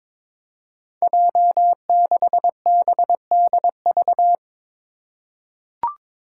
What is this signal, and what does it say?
Morse code, heard as a keyed tone, about 22 wpm, 700 Hz. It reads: J6BDV
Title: Five Challenging Letter-Number Combinations 22wpm